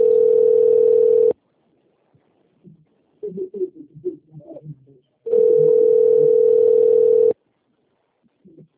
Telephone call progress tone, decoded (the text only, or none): ringback tone